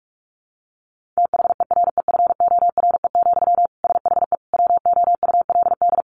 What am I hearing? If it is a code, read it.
Morse code, heard as a keyed tone, about 44 wpm, 700 Hz. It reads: THEREFORE, SHE WOULD
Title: Sentences from Top 800 Words 45wpm